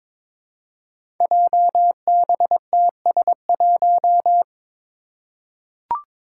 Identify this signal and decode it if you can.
Morse code, heard as a keyed tone, about 22 wpm, 700 Hz. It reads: JBTS1
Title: Five Challenging Letter-Number Combinations 22wpm